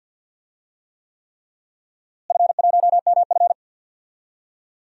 Morse code, heard as a keyed tone, about 46 wpm, 700 Hz. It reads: F1MF
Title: DXCC-100 Call Signs - 4 Characters - Repeated 3x - 50wpm